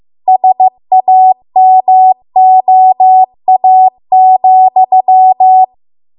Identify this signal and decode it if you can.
Morse code, heard as a keyed tone, about 15 wpm, 750 Hz. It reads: SAMOA,